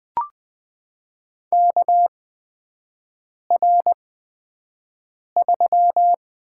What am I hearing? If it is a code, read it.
Morse code, heard as a keyed tone, about 21 wpm, spaced as for 5 wpm, 700 Hz. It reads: KR3
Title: USA Call Signs - 6 Characters - 8x Character Spacing - 20wpm